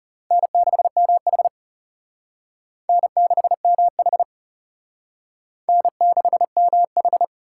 Morse code, heard as a keyed tone, about 34 wpm, 700 Hz. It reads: N6MH N6MH N6MH
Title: Speed Racing - USA 1x2 Call Signs 30wpm